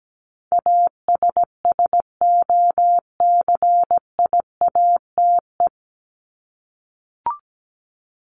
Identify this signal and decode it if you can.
Morse code, heard as a keyed tone, about 17 wpm, 700 Hz. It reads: ASSOCIATE
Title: ED Suffix - Top 100 Words with Base Words 17wpm